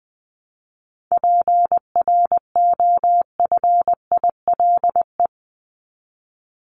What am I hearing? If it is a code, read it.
Morse code, heard as a keyed tone, about 20 wpm, 700 Hz. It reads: PROFILE
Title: Top 1501-1750 Words 20wpm